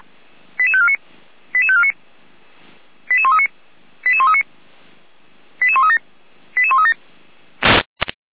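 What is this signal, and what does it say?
Five-tone selective call (ZVEI), heard as quick runs of five tones, tones of about 70 ms. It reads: 88549 88549 88139 88139 88137 88137
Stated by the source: Five-tone ZVEI selective call received by SDR.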